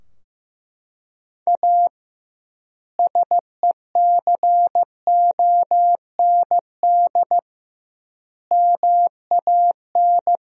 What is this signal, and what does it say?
Morse code, heard as a keyed tone, about 15 wpm, 700 Hz. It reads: A SECOND MAN
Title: 2x Word-Spacing - Sentences from Top 300 Words - 15wpm